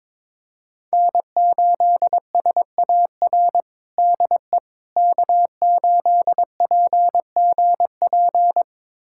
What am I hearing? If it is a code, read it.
Morse code, heard as a keyed tone, about 22 wpm, 700 Hz. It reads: N8SAR DE K8PGP
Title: USA Callsign DE Callsign - 5 Characters - 22wpm